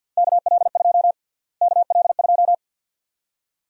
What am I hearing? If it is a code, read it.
Morse code, heard as a keyed tone, about 49 wpm, 700 Hz. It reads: KL2 KL2
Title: USA Call Sign Prefixes - Repeated 2x - 50wpm